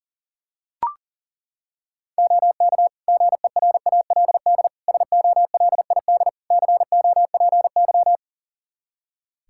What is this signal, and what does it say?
Morse code, heard as a keyed tone, about 37 wpm, 700 Hz. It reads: OK GERALD SOLID COPY
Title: Practice QSO - Exchange 3 - More Info - 40wpm